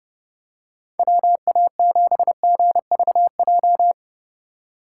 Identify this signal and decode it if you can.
Morse code, heard as a keyed tone, about 30 wpm, 700 Hz. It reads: WA7GVJ
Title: USA Call Signs - 6 Characters - 2x Character Spacing - 30wpm